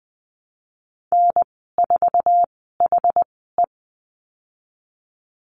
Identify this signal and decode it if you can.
Morse code, heard as a keyed tone, about 20 wpm, spaced as for 15 wpm, 700 Hz. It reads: N4HE
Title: 147 Lesson 24g - Letter y - Call Signs 20wpm